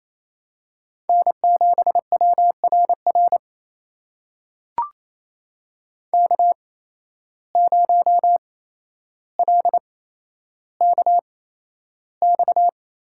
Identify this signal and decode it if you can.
Morse code, heard as a keyed tone, about 28 wpm, spaced as for 12 wpm, 700 Hz. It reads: N7WRR K0LKX
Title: USA Call Signs - 5 Characters - 8x Character Spacing - 28wpm